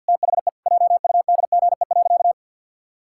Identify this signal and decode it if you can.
Morse code, heard as a keyed tone, about 53 wpm, 700 Hz. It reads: THE JUDGE.
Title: Pangram Challenge v2 - 50wpm